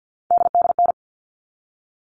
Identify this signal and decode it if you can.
Morse code, heard as a keyed tone, about 50 wpm, 700 Hz. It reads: DDN
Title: All Cut Numbers - 3-Digit 50wpm